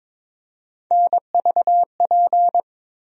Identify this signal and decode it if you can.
Morse code, heard as a keyed tone, about 22 wpm, 700 Hz. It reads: NVP
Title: USA Call Sign Postfixes - 3 Letters 22wpm